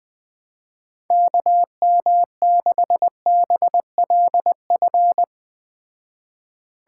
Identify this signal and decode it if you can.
Morse code, heard as a keyed tone, about 20 wpm, 700 Hz. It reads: KM6BLF
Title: Speed Racing - USA 2x3 Call Signs 17wpm